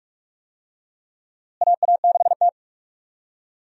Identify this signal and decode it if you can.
Morse code, heard as a keyed tone, about 42 wpm, 700 Hz. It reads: AA6T
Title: USA Call Signs - 4 Characters - Repeated 2x - 45wpm